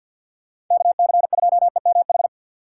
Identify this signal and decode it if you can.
Morse code, heard as a keyed tone, about 53 wpm, 700 Hz. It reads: KC2EMH
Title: USA Call Signs - 6 Characters - 6x Character Spacing - 50wpm